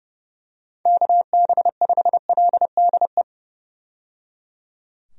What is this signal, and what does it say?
Morse code, heard as a keyed tone, about 31 wpm, 700 Hz. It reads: KB5LDE
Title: USA Call Signs - 6 Characters - 30wpm